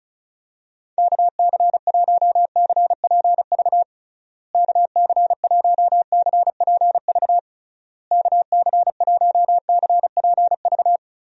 Morse code, heard as a keyed tone, about 36 wpm, 700 Hz. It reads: KC1CPV KC1CPV KC1CPV
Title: USA Call Signs - 6 Characters - Repeated 3x - 35wpm